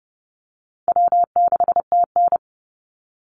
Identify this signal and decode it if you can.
Morse code, heard as a keyed tone, about 30 wpm, 700 Hz. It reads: W6TN
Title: Easy Call Signs Repeated 2x - 30wpm